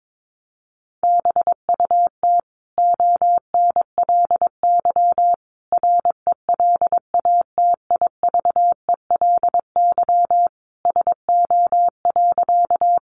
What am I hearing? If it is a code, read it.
Morse code, heard as a keyed tone, about 22 wpm, 700 Hz. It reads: BUT ONLY RELATIVELY SO.
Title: Sentences from Top 1100 Words 22wpm